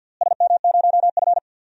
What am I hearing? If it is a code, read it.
Morse code, heard as a keyed tone, about 52 wpm, 700 Hz. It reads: SM0F
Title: DXCC-100 Call Signs - 4 Characters - Repeated 2x - 50wpm